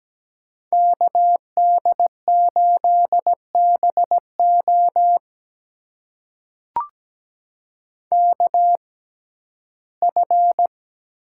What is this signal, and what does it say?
Morse code, heard as a keyed tone, about 17 wpm, spaced as for 12 wpm, 700 Hz. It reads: KD8BO KF
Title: USA Call Signs - 5 Characters - 6x Character Spacing - 17wpm